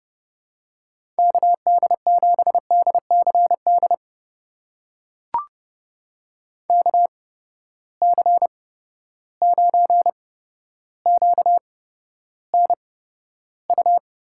Morse code, heard as a keyed tone, about 30 wpm, spaced as for 13 wpm, 700 Hz. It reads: KD7DCD KC9QNU
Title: USA Call Signs - 6 Characters - 8x Character Spacing - 30wpm